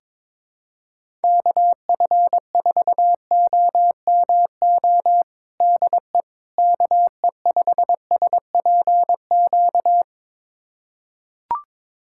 Morse code, heard as a keyed tone, about 21 wpm, 700 Hz. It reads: KF4OMO DE KE5SPQ
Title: USA Callsign DE Callsign - 6 Characters - 22wpm